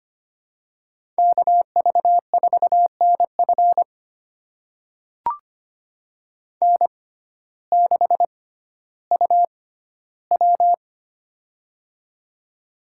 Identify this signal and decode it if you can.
Morse code, heard as a keyed tone, about 25 wpm, spaced as for 14 wpm, 700 Hz. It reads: KV4NF N6UW
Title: USA Call Signs - 6x Character Spacing - 25wpm